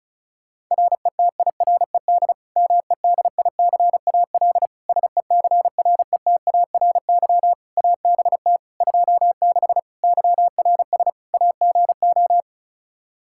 Morse code, heard as a keyed tone, about 36 wpm, 700 Hz. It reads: RETIRED MEDICAL SECRETARY ABT 26 YRS AGO
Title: K1USN SST QSO Practice 35wpm